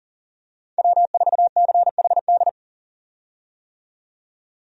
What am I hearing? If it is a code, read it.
Morse code, heard as a keyed tone, about 40 wpm, 700 Hz. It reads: W4CHD
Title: USA Call Signs - 5 Characters - 40wpm